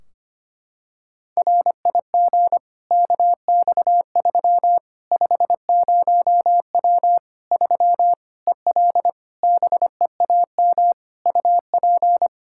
Morse code, heard as a keyed tone, about 25 wpm, 700 Hz. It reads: RIG KX3 50W 3 EL BEAM UP
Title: Practice QSO - Exchange 2 - Info - 25wpm